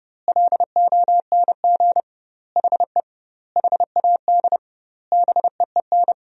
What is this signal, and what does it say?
Morse code, heard as a keyed tone, about 30 wpm, 700 Hz. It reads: LONG HE HAD BEEN
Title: 2x Word-Spacing - Sentences from Top 300 Words - 30wpm